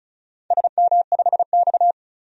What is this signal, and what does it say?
Morse code, heard as a keyed tone, about 34 wpm, 700 Hz. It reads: SM5X
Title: DXCC-100 Call Signs - 4 Characters - 35wpm